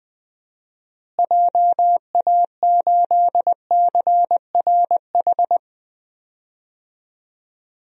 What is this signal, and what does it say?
Morse code, heard as a keyed tone, about 20 wpm, 700 Hz. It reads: JA8CRH
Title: DXCC-100 Call Signs - 6 Characters - Repeated 3x - 20wpm